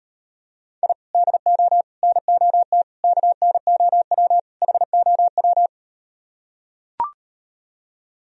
Morse code, heard as a keyed tone, about 38 wpm, 700 Hz. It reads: I DO NOT KNOW HOW